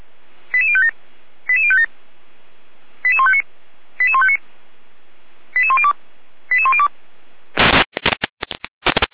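Five-tone selective call (ZVEI), heard as quick runs of five tones, tones of about 70 ms. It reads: 88057 88057 88159 88159 88102 88102
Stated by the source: ZVEI five-tone selective call received on an SDR.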